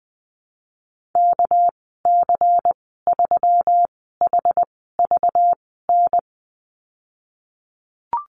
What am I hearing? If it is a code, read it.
Morse code, heard as a keyed tone, about 20 wpm, spaced as for 15 wpm, 700 Hz. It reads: KC3HVN
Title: 213 Lesson 34g - Number 8 - Call Signs 20wpm